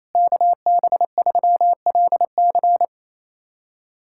Morse code, heard as a keyed tone, about 28 wpm, 700 Hz. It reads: KB3LC
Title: USA Callsign DE Callsign - 5 Characters - 28wpm